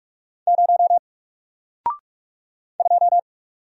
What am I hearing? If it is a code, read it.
Morse code, heard as a keyed tone, about 44 wpm, 700 Hz. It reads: 0 2
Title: Single Character (Letter:Number:Punctuation:Prosign) Alt 45wpm